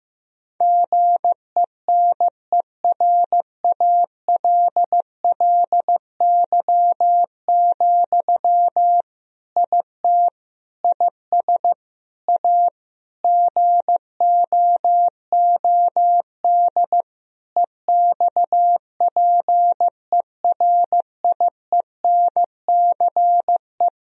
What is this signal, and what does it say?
Morse code, heard as a keyed tone, about 15 wpm, 700 Hz. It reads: GENERALLY, IT IS A GOOD EXPERIENCE
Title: Sentences from Top 1000 Words - Review Part 4 - 15wpm